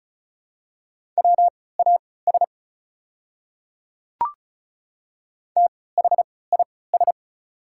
Morse code, heard as a keyed tone, about 34 wpm, spaced as for 20 wpm, 700 Hz. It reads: WAS THIS
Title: 3x Character Spacing - Top 300 Words - 35wpm